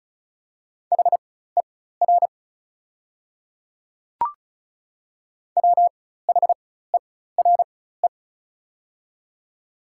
Morse code, heard as a keyed tone, about 36 wpm, spaced as for 16 wpm, 700 Hz. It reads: HER WHERE
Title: 4x Character Spacing - Top 200 Words - 35wpm